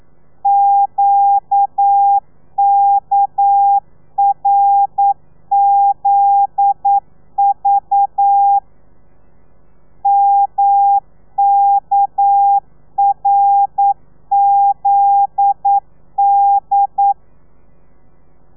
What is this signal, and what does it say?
Morse code, heard as a keyed tone, about 9 wpm, 800 Hz. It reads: QKRZV MKRZD
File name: telegrafia / i2tfj / Lez9 40